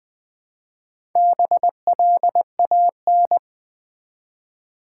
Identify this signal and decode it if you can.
Morse code, heard as a keyed tone, about 20 wpm, 700 Hz. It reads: BLAN
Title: Build A Word - 1 Letter At A Time - 5-Letter Words - 20wpm